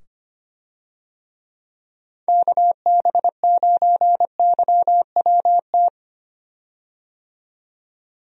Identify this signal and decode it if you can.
Morse code, heard as a keyed tone, about 25 wpm, 700 Hz. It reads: KB9YWT
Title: USA Call Signs - 6 Characters - 25wpm